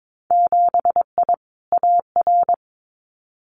Morse code, heard as a keyed tone, about 22 wpm, 700 Hz. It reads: 7I AR
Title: Winter Field Day Exchanges Encore 22wpm